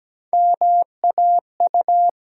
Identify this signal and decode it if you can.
Morse code, heard as a keyed tone, about 17 wpm, 700 Hz. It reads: MAU
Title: USA Call Sign Postfixes - 3 Letters 17wpm